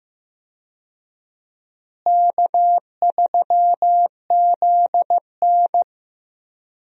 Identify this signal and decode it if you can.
Morse code, heard as a keyed tone, about 15 wpm, 700 Hz. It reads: K3ZN
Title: USA Call Signs - 4 Characters - Repeated 2x - 15wpm